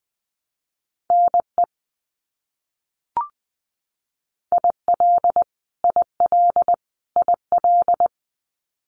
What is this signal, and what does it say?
Morse code, heard as a keyed tone, about 20 wpm, 700 Hz. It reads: NE IL IL IL
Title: US States Repeated 3x 20wpm